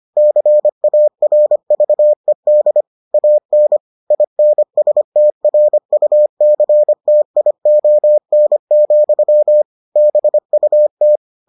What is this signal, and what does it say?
Morse code, heard as a keyed tone, about 25 wpm, 600 Hz. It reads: CARVED AN INSTRUCTION, BUT